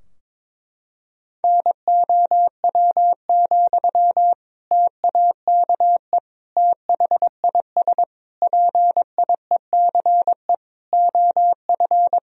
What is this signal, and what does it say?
Morse code, heard as a keyed tone, about 22 wpm, 700 Hz. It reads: NOW, TAKE THIS PIECE OF
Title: Sentences from Top 1000 Words - Review Part 2 - 22wpm